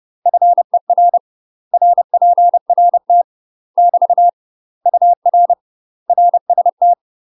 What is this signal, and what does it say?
Morse code, heard as a keyed tone, about 30 wpm, 700 Hz. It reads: FER RPRT <BT> UR RST
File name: UK MASTER QSOs TRAINING / 30WPM / UK MASTER QSOs 0016